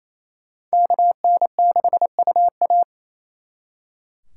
Morse code, heard as a keyed tone, about 36 wpm, 700 Hz. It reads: KN6UA